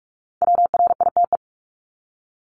Morse code, heard as a keyed tone, about 44 wpm, 700 Hz. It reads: WRITE
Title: ING Suffix - Top 100 Words with Base Words 45wpm